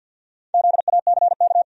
Morse code, heard as a keyed tone, about 52 wpm, 700 Hz. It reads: ZACK
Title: USA Common Names - Max 4 Letters - 50wpm